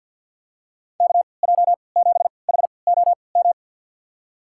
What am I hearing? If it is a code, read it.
Morse code, heard as a keyed tone, about 51 wpm, spaced as for 28 wpm, 700 Hz. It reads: KJ7HOM
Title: USA Call Signs - 3x Character Spacing - 50wpm